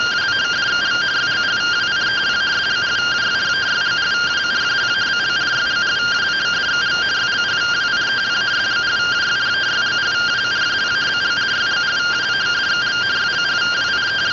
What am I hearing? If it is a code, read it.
Two-tone teleprinter signal, about 170 Hz shift.
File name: ss-portrait.gen-12 rtty rx